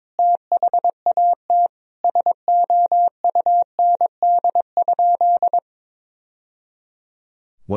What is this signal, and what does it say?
Morse code, heard as a keyed tone, about 22 wpm, 700 Hz. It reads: THAT SOUND?
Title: Sentences from Top 1000 Words - Review Part 1 - 22wpm